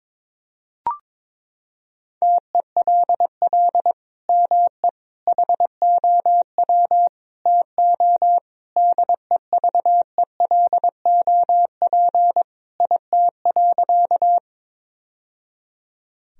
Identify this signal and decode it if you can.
Morse code, heard as a keyed tone, about 22 wpm, 700 Hz. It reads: TELL ME HOW TO DEVELOP IT.
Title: Sentences from Top 1000 Words - Review Part 2 - 22wpm